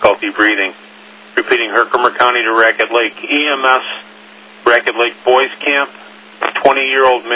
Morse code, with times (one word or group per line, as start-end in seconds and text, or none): none